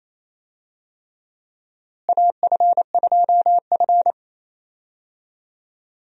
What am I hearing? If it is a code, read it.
Morse code, heard as a keyed tone, about 28 wpm, 700 Hz. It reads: AF2F
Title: USA Call Signs - 4 Characters - Repeated 3x - 28wpm